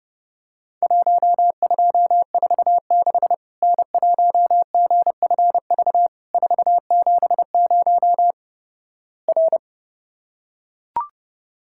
Morse code, heard as a keyed tone, about 30 wpm, 700 Hz. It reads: 1246 N1GFV 470 R
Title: Worked All Europe DX Contest - QTC Exchange - 30wpm